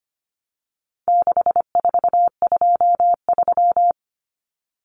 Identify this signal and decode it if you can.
Morse code, heard as a keyed tone, about 25 wpm, 700 Hz. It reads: 6423
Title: Numbers - 4-Digit 25wpm